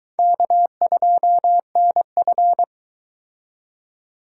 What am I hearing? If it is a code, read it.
Morse code, heard as a keyed tone, about 22 wpm, 700 Hz. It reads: K2NF